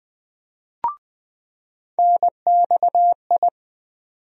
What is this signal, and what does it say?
Morse code, heard as a keyed tone, about 20 wpm, 700 Hz. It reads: NXI
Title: USA Call Sign Postfixes - 3 Letters 20wpm